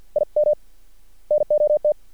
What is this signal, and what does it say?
Morse code, heard as a keyed tone, about 50 wpm, 600 Hz. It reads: I M NOT
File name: archived / CRSnet